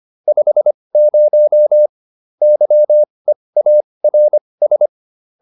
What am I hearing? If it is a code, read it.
Morse code, heard as a keyed tone, about 25 wpm, spaced as for 20 wpm, 600 Hz. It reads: 50 YEARS
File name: SS206 20